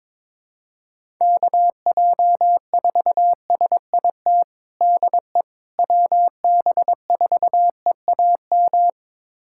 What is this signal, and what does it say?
Morse code, heard as a keyed tone, about 22 wpm, 700 Hz. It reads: KJ4SIT DE WB4EAM
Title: USA Callsign DE Callsign - 6 Characters - 22wpm